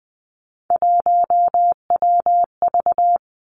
Morse code, heard as a keyed tone, about 20 wpm, 700 Hz. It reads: 1WV
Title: Three Challenging Letter-Number Combinations 20wpm